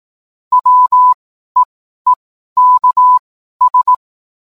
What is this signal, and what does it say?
Morse code, heard as a keyed tone, about 18 wpm, spaced as for 12 wpm, 1000 Hz. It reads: WEEKS
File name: cwblog / cw mp3 directory / mem morse 12